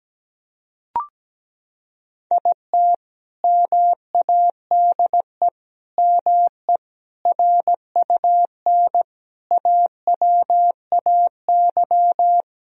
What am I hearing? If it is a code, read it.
Morse code, heard as a keyed tone, about 17 wpm, 700 Hz. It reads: IT MADE ME RUN AWAY
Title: Sentences from Top 1000 Words - Review Part 1 - 17wpm